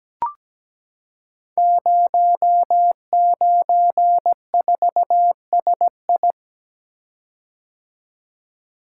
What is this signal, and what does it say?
Morse code, heard as a keyed tone, about 17 wpm, 700 Hz. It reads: 094SI
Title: Code Groups - 5 Letter:Numbers - 17wpm